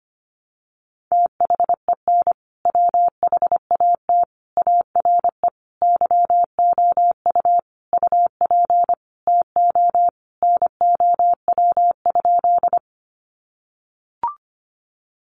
Morse code, heard as a keyed tone, about 25 wpm, 700 Hz. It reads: THEN WHAT ARE YOU UP TO NOW?
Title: Sentences from Top 100 Words 25wpm